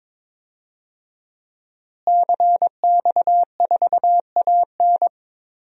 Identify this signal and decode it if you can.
Morse code, heard as a keyed tone, about 21 wpm, 700 Hz. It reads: CX4AN
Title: DXCC-100 Call Signs - 5 Characters - Repeated 2x - 22wpm